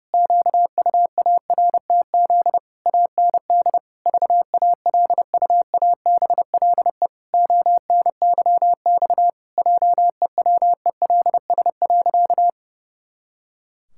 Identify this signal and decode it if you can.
Morse code, heard as a keyed tone, about 30 wpm, 700 Hz. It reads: QUARTZ AND VALUABLE ONYX JEWELS.
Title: Pangram Challenge v2 - 30wpm